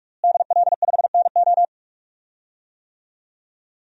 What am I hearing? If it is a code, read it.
Morse code, heard as a keyed tone, about 45 wpm, 700 Hz. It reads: DL5NO
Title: DXCC-100 Call Signs - 5 Characters - Repeated 3x - 45wpm